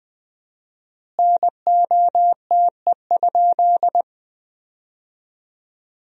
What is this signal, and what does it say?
Morse code, heard as a keyed tone, about 21 wpm, 700 Hz. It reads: NOTE?
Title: 4x Word-Spacing - Sentences from Top 500 Words - 20wpm